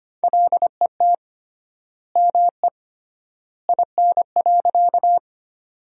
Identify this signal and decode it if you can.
Morse code, heard as a keyed tone, about 25 wpm, 700 Hz. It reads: LET ME IN.
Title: 3x Word-Spacing - Sentences from Top 300 Words - 25wpm